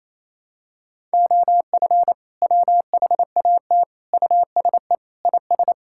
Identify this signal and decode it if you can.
Morse code, heard as a keyed tone, about 28 wpm, 700 Hz. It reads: OF WHAT USE IS
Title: Sentences from Top 1000 Words - Review Part 5 - 28wpm